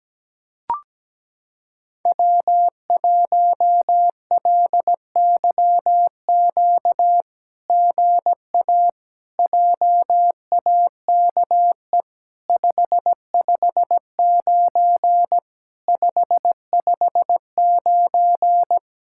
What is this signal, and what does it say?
Morse code, heard as a keyed tone, about 17 wpm, 700 Hz. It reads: W1LYQ GA JAKE 559 559
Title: SOTA QSO Practice 17wpm